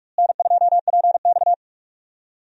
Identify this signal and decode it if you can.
Morse code, heard as a keyed tone, about 46 wpm, 700 Hz. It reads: N2PX
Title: USA Call Signs - 4 Characters - Repeated 3x - 45wpm